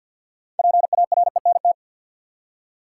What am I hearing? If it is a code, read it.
Morse code, heard as a keyed tone, about 48 wpm, 700 Hz. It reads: PARENT
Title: Six Letter Words - 50wpm